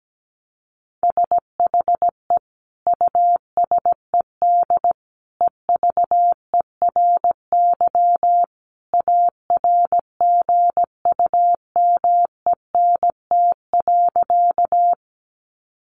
Morse code, heard as a keyed tone, about 17 wpm, 700 Hz. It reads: SHE USED EVERY ARGUMENT.
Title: Sentences from Top 1400 Words 17wpm